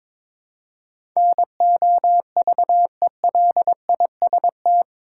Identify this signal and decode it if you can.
Morse code, heard as a keyed tone, about 22 wpm, 700 Hz. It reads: NOVELIST
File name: practice / 22wpm / Eight Letter Words - 22wpm